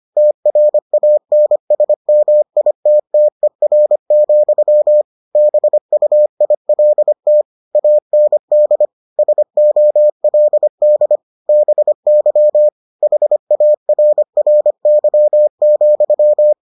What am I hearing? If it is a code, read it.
Morse code, heard as a keyed tone, about 25 wpm, 600 Hz. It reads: TRANSMITTER, BUILT AND SOLD BY HARRY,